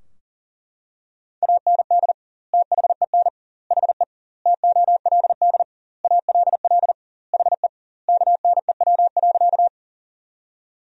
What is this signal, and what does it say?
Morse code, heard as a keyed tone, about 40 wpm, 700 Hz. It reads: AND THEN HE TOLD ALL HE KNEW.
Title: 2x Word-Spacing - Sentences from Top 400 Words - 40wpm